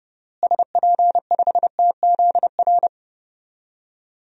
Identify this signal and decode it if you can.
Morse code, heard as a keyed tone, about 30 wpm, 700 Hz. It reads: SP5TZR
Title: DXCC-100 Call Signs - 6 Characters - Repeated 3x - 30wpm